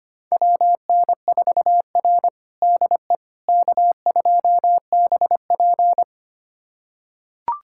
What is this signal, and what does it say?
Morse code, heard as a keyed tone, about 25 wpm, 700 Hz. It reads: WN4R DE K2BP
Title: USA Callsign DE Callsign - 4 Characters - 25wpm